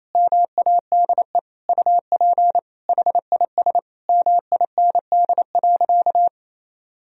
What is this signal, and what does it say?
Morse code, heard as a keyed tone, about 27 wpm, 700 Hz. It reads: MADE UP HIS MIND.
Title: Sentences from Top 1000 Words - Review Part 2 - 28wpm